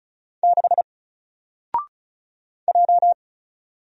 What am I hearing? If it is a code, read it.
Morse code, heard as a keyed tone, about 35 wpm, 700 Hz. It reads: 6 J
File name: Single-Letter-Number-35wpm